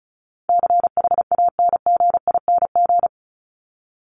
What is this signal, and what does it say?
Morse code, heard as a keyed tone, about 34 wpm, 700 Hz. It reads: CHANGING
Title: ING Suffix - Top 100 Words with Base Words 35wpm